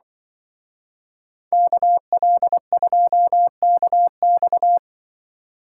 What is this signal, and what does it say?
Morse code, heard as a keyed tone, about 25 wpm, 700 Hz. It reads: KL2KX
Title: Speed Racing - USA 2x2 Call Signs 20wpm